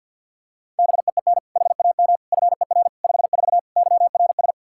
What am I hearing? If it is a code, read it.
Morse code, heard as a keyed tone, about 51 wpm, 700 Hz. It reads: BEEN HAM FER 54 YRS
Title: K1USN SST QSO Practice 50wpm